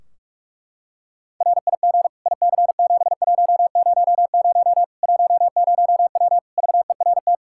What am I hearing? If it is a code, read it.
Morse code, heard as a keyed tone, about 46 wpm, 700 Hz. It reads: RIG IC7100 10W VERT
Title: Practice QSO - Exchange 2 - Info - 45wpm